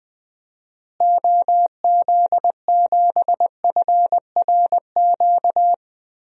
Morse code, heard as a keyed tone, about 20 wpm, 700 Hz. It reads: OZ7FRQ
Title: DXCC-100 Call Signs - 6 Characters - Repeated 3x - 20wpm